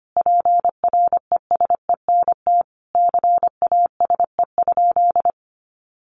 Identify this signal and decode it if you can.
Morse code, heard as a keyed tone, about 25 wpm, 700 Hz. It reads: PRESENT CASE?
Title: Sentences from Top 1200 Words 25wpm